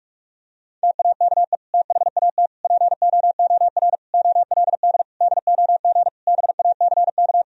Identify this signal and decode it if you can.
Morse code, heard as a keyed tone, about 46 wpm, 700 Hz. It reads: TAKE THAT POOR OLD DOG BACK
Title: Sentences from Top 1000 Words - Review Part 2 - 45wpm